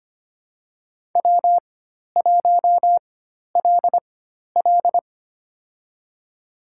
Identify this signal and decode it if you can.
Morse code, heard as a keyed tone, about 25 wpm, spaced as for 12 wpm, 700 Hz. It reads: W1LL
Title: USA Call Signs - 4 Characters - 4x Character Spacing - 25wpm